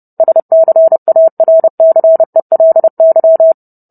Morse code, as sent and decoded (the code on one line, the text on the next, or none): ... -.-. .- .-. -.-. .-.. -.--
SCARCELY